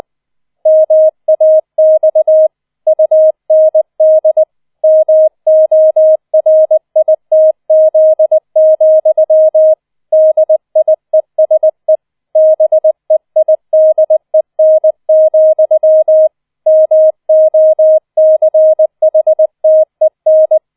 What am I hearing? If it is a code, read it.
Morse code, heard as a keyed tone, about 20 wpm, 630 Hz. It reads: MAX UND MORITZ, DIESE BEIDEN, MOCHTEN
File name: 004 Vierter Streich 20wpm